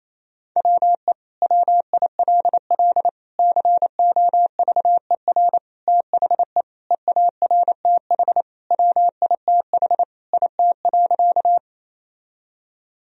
Morse code, heard as a keyed tone, about 28 wpm, 700 Hz. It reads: WE WILL COVER THE EARTH WITH IT.
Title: Sentences from Top 1000 Words - Review Part 1 - 28wpm